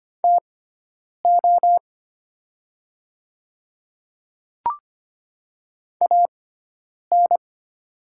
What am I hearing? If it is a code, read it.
Morse code, heard as a keyed tone, about 25 wpm, spaced as for 9 wpm, 700 Hz. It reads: TO AN
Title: 6x Character Spacing - Top 100 Words - 25wpm